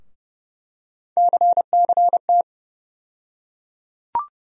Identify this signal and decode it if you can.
Morse code, heard as a keyed tone, about 30 wpm, 700 Hz. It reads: CCT
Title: USA Call Sign Postfixes - 3 Letters 30wpm